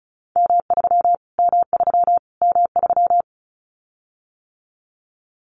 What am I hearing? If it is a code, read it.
Morse code, heard as a keyed tone, about 34 wpm, 700 Hz. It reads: M3 M3 M3
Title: Callsign Prefixes for Top 25 DXCC - Repeated 3x - 35wpm